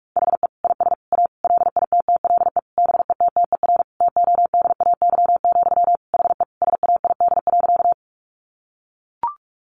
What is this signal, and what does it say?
Morse code, heard as a keyed tone, about 44 wpm, 700 Hz. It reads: HE IS A LITTLE BETTER TODAY, HE SAID.
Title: Sentences from Top 600 Words 45wpm